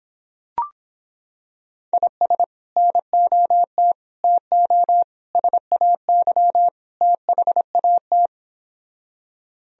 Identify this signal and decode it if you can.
Morse code, heard as a keyed tone, about 26 wpm, 700 Hz. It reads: IS NOT TO SAY THAT